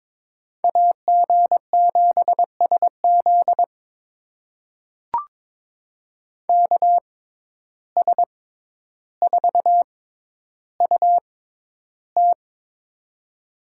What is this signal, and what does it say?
Morse code, heard as a keyed tone, about 22 wpm, spaced as for 11 wpm, 700 Hz. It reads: AG7SZ KS4UT
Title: USA Call Signs - 5 Characters - 6x Character Spacing - 22wpm